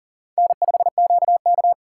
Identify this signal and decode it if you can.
Morse code, heard as a keyed tone, about 50 wpm, 700 Hz. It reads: N5QK
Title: Speed Racing - USA 1x2 Call Signs 40wpm